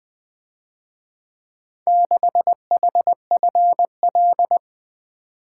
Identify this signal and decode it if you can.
Morse code, heard as a keyed tone, about 22 wpm, 700 Hz. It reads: 6HFL
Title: Speed-Racing - Four Letter-Number Combinations - 20wpm